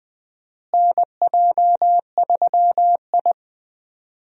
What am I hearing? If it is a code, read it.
Morse code, heard as a keyed tone, about 20 wpm, 700 Hz. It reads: NJ3I
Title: USA Call Signs - 4 Characters - Repeated 2x - 20wpm